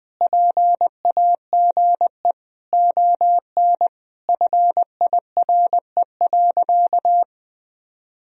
Morse code, heard as a keyed tone, about 20 wpm, 700 Hz. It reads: PAGE ON FIRE.
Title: Sentences from Top 1000 Words - Review Part 1 - 20wpm